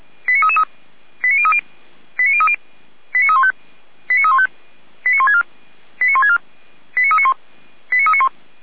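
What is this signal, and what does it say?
Five-tone selective call (ZVEI), heard as quick runs of five tones, tones of about 70 ms. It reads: 89333 89030 89030 89315 89315 89164 89164 89391 89391
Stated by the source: ZVEI five-tone call received by SDR, from audio.f85360.feuerwehr-orlamuende.org (Feuerwehr Orlamünde).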